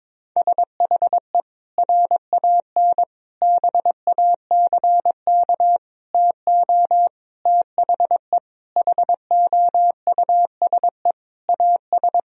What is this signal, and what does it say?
Morse code, heard as a keyed tone, about 22 wpm, 700 Hz. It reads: SHE RAN BACK TO THE HOUSE AS
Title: Sentences from Top 1000 Words - Review Part 3 - 22wpm